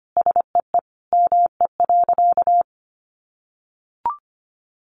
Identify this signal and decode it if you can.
Morse code, heard as a keyed tone, about 25 wpm, 700 Hz. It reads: SEE ME.
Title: Sentences from Top 1100 Words 25wpm